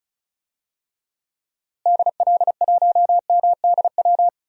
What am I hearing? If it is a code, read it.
Morse code, heard as a keyed tone, about 34 wpm, 700 Hz. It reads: DL1MDW
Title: DXCC-100 Call Signs - 6 Characters - Repeated 2x - 35wpm